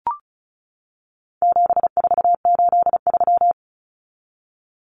Morse code, heard as a keyed tone, about 34 wpm, 700 Hz. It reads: 7483
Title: Common Cut Numbers - 4-Digit 35wpm